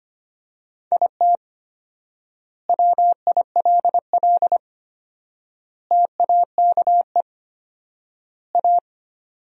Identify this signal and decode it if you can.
Morse code, heard as a keyed tone, about 25 wpm, 700 Hz. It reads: IT WILL TAKE A
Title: 4x Word-Spacing - Sentences from Top 200 Words - 25wpm